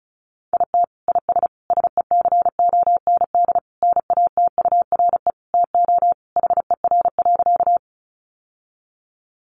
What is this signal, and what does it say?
Morse code, heard as a keyed tone, about 34 wpm, 700 Hz. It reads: IT IS SECOND NATURE TO HER.
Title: Sentences from Top 600 Words 35wpm